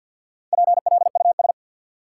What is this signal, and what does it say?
Morse code, heard as a keyed tone, about 52 wpm, 700 Hz. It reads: PLUS
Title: Four Letter Words - 50wpm